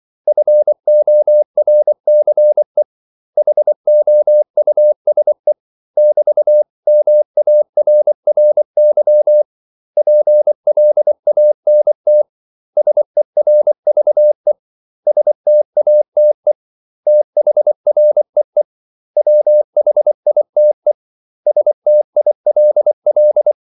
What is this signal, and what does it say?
Morse code, heard as a keyed tone, about 24 wpm, 600 Hz. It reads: FORCE HOUSE <BT>MARRY PLANT SERVE STATE THREE WHITE STILL
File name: CW / Practice / 500MostCommonEnglishWords24